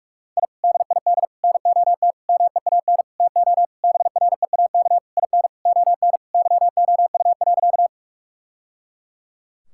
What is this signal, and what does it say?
Morse code, heard as a keyed tone, about 45 wpm, 700 Hz. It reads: I DID NOT MEAN TO BREAK IN ON YOU.
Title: Sentences from Top 1000 Words - Review Part 6 - 45wpm